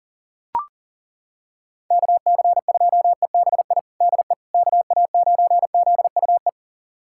Morse code, heard as a keyed tone, about 40 wpm, 700 Hz. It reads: KC2EBI DE KA9ZUE
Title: USA Callsign DE Callsign 40wpm